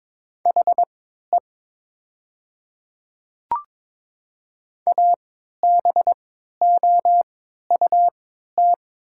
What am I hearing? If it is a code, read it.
Morse code, heard as a keyed tone, about 22 wpm, spaced as for 12 wpm, 700 Hz. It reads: HE ABOUT
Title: 3x Character Spacing - Top 100 Words - 22wpm